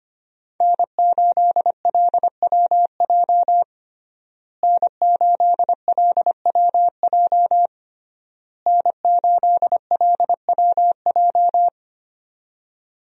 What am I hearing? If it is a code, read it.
Morse code, heard as a keyed tone, about 25 wpm, 700 Hz. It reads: N8LWJ N8LWJ N8LWJ
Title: USA Call Signs - 5 Characters - Repeated 3x - 25wpm